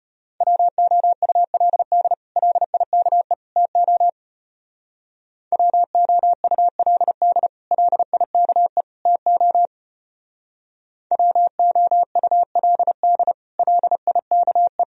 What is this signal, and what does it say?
Morse code, heard as a keyed tone, about 31 wpm, 700 Hz. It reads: WOULD LIKE TO WOULD LIKE TO WOULD LIKE
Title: Speed-Racing - Sets of 3 Words 25wpm